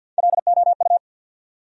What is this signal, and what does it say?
Morse code, heard as a keyed tone, about 52 wpm, 700 Hz. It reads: LOU
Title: USA Common Names - Max 4 Letters - 50wpm